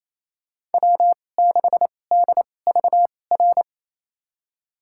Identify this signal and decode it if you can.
Morse code, heard as a keyed tone, about 28 wpm, spaced as for 20 wpm, 700 Hz. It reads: W6DVR
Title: USA Call Signs - 5 Characters - 2x Character Spacing - 28wpm